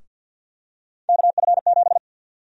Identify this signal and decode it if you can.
Morse code, heard as a keyed tone, about 50 wpm, 700 Hz. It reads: KF7
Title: USA Call Sign Prefixes - 50wpm